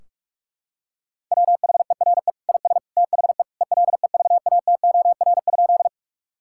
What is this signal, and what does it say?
Morse code, heard as a keyed tone, about 46 wpm, 700 Hz. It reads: WHERE IS THE ELEVATOR?
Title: Sentences from Top 1600 Words 45wpm